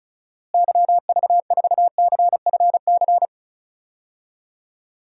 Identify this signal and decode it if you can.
Morse code, heard as a keyed tone, about 34 wpm, 700 Hz. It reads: YV4CFC
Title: DXCC-100 Call Signs - 6 Characters - Repeated 2x - 35wpm